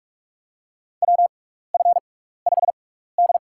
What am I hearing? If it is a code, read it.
Morse code, heard as a keyed tone, about 46 wpm, spaced as for 16 wpm, 700 Hz. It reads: WF5D
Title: USA Call Signs - 4 Characters - 6x Character Spacing - 45wpm